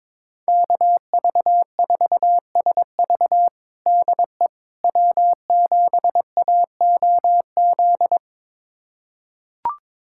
Morse code, heard as a keyed tone, about 22 wpm, 700 Hz. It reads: KV4SV DE W7AOZ
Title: USA Callsign DE Callsign - 5 Characters - 22wpm